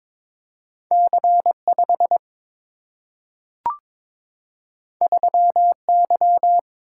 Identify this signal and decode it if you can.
Morse code, heard as a keyed tone, about 22 wpm, 700 Hz. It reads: C5 3Y
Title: Single-Letter Single-Number 22wpm